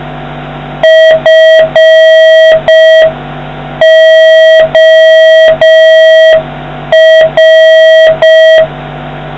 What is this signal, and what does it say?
Morse code, heard as a keyed tone, about 5 wpm, 1950 Hz, with noise. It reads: FOR